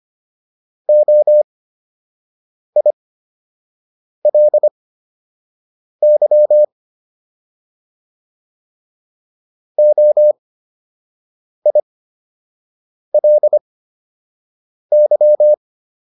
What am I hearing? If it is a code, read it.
Morse code, heard as a keyed tone, about 25 wpm, spaced as for 6 wpm, 600 Hz. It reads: OILY OILY